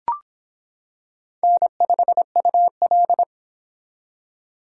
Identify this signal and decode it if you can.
Morse code, heard as a keyed tone, about 25 wpm, 700 Hz. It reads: N5UL